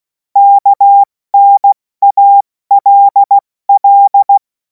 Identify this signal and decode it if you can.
Morse code, heard as a keyed tone, about 16 wpm, spaced as for 14 wpm, 800 Hz. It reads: KNALL
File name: Morsekurs / Witze / Witz 17. Februar 2026 70 F80